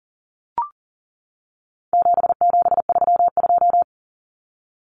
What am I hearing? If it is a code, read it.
Morse code, heard as a keyed tone, about 40 wpm, 700 Hz. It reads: 7732
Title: Common Cut Numbers - 4-Digit 40wpm